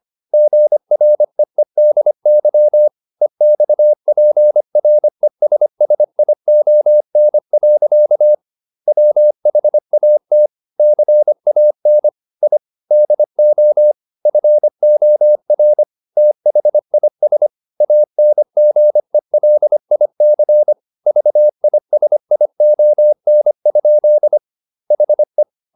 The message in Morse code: GREEDY EXPRESSION. WHAT CAN I DO FOR THIS ANGELIC VISION? HE

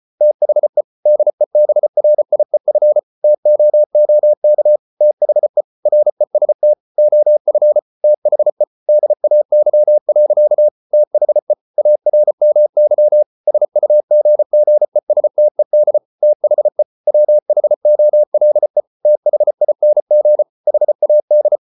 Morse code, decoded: THE DEBRIEF TOOK THE REST OF THE DAY. THE ARMY SUGGESTED THE WHOLE THING HAD

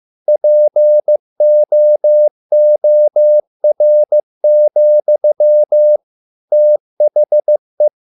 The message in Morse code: POOR, THE

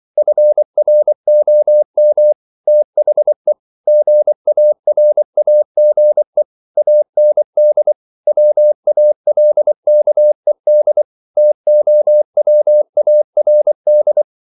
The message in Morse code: FROM THE GARAGE AND WALKED TOWARD

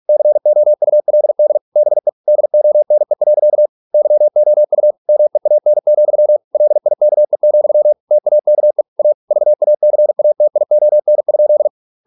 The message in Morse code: COULD BE DONE. YOU MEAN, LIKE, TAKE A VACATION?

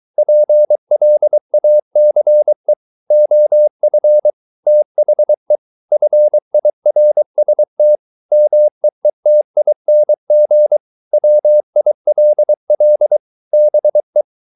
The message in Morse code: PLACE OF THE FIRST MEETING WILL BE

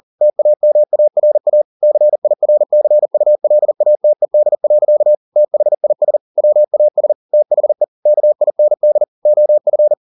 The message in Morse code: TAMARA CIRCULATED. THIS WAS THE KIND OF

